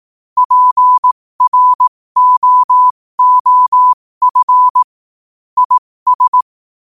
Morse code, heard as a keyed tone, about 18 wpm, spaced as for 15 wpm, 1000 Hz. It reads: PROOF IS
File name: qotdfiles / 15wpm